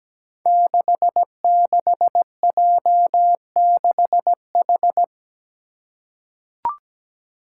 Morse code, heard as a keyed tone, about 17 wpm, 700 Hz. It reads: 66J6H
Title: Five Challenging Letter-Number Combinations 17wpm